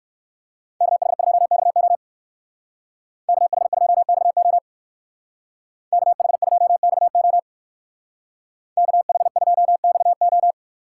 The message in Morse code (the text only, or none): KH2XO KH2XO KH2XO KH2XO